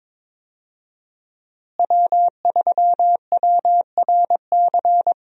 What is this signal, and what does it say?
Morse code, heard as a keyed tone, about 22 wpm, 700 Hz. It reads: W3WRC
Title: USA Call Signs - 5 Characters - Repeated 2x - 22wpm